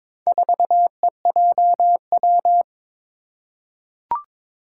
Morse code, heard as a keyed tone, about 22 wpm, 700 Hz. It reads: 4EJW
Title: Four Challenging Letter-Number Combinations 22wpm